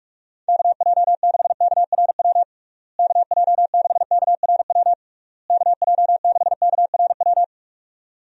Morse code, heard as a keyed tone, about 42 wpm, 700 Hz. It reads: KJ6KRW KJ6KRW KJ6KRW
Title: USA Call Signs - 6 Characters - Repeated 3x - 45wpm